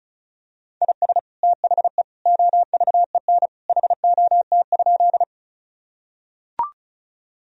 Morse code, heard as a keyed tone, about 36 wpm, 700 Hz. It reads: IS THE OVEN HOT?
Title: Sentences from Top 1000 Words - Review Part 4 - 35wpm